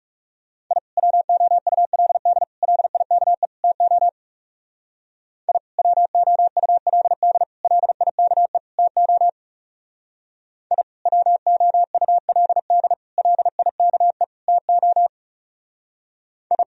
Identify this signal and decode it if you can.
Morse code, heard as a keyed tone, about 40 wpm, 700 Hz. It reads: I WOULD LIKE TO I WOULD LIKE TO I WOULD LIKE TO I